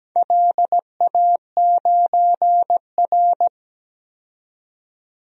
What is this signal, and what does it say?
Morse code, heard as a keyed tone, about 17 wpm, 700 Hz. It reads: LA9R
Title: DXCC-100 Call Signs - 4 Characters - 17wpm